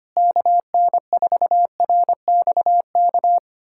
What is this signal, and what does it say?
Morse code, heard as a keyed tone, about 25 wpm, 700 Hz. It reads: KN4RXK
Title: USA Callsign DE Callsign 25wpm